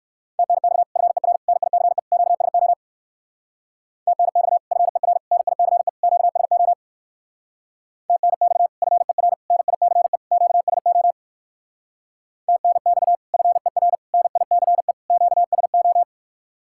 Morse code, heard as a keyed tone, about 59 wpm, 700 Hz. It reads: TNX FER NICE QSO TNX FER NICE QSO TNX FER NICE QSO TNX FER NICE QSO